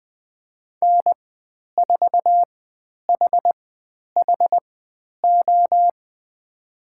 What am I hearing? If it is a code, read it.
Morse code, heard as a keyed tone, about 20 wpm, spaced as for 10 wpm, 700 Hz. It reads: N4HHO
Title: 095 Lesson 16g - Letter c - Call Signs 20wpm